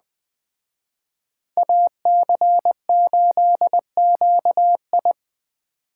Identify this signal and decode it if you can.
Morse code, heard as a keyed tone, about 20 wpm, 700 Hz. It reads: AC8QI
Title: Speed Racing - USA 2x2 Call Signs 17wpm